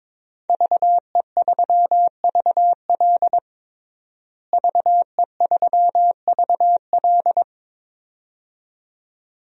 Morse code, heard as a keyed tone, about 21 wpm, 700 Hz. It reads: VE3VL VE3VL
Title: DXCC-100 Call Signs - 5 Characters - Repeated 2x - 22wpm